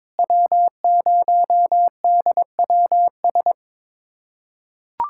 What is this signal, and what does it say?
Morse code, heard as a keyed tone, about 22 wpm, 700 Hz. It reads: W0DWS
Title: USA Callsign DE Callsign - 5 Characters - 22wpm